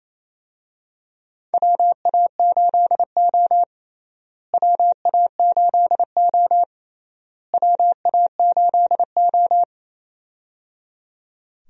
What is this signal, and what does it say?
Morse code, heard as a keyed tone, about 28 wpm, 700 Hz. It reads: WA8O WA8O WA8O
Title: USA Call Signs - 4 Characters - Repeated 3x - 28wpm